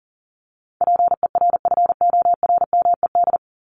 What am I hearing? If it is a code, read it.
Morse code, heard as a keyed tone, about 40 wpm, 700 Hz. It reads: PERFORMED
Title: ED Suffix - Top 100 Words with Base Words 40wpm